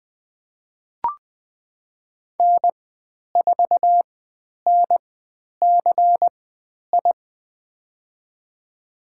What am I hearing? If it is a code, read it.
Morse code, heard as a keyed tone, about 20 wpm, spaced as for 10 wpm, 700 Hz. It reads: N4NCI
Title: 109 Lesson 18g - Letter m - Call Signs 20wpm